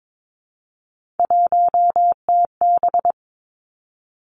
Morse code, heard as a keyed tone, about 22 wpm, 700 Hz. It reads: 1TB
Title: Three Challenging Letter-Number Combinations 22wpm